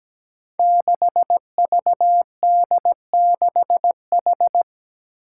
Morse code, heard as a keyed tone, about 17 wpm, 700 Hz. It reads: 6VD6H
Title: Five Challenging Letter-Number Combinations 17wpm